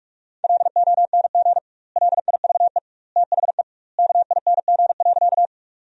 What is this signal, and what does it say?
Morse code, heard as a keyed tone, about 46 wpm, 700 Hz. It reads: LONG LIVE THE KING.
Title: 2x Word-Spacing - Sentences from Top 400 Words - 45wpm